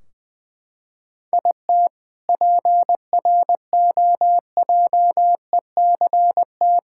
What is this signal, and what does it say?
Morse code, heard as a keyed tone, about 20 wpm, 700 Hz. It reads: IT PROJECT
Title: K1USN SST QSO Practice 20wpm